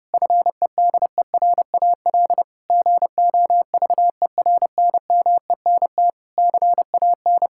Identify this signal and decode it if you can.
Morse code, heard as a keyed tone, about 32 wpm, 700 Hz. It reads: FEDERAL GOVERNMENT CAN
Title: Sentences from Top 1000 Words - Review Part 5 - 30wpm